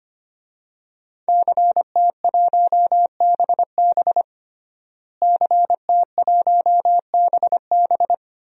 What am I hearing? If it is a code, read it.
Morse code, heard as a keyed tone, about 25 wpm, 700 Hz. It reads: CT1BB CT1BB
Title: DXCC-100 Call Signs - 5 Characters - Repeated 2x - 25wpm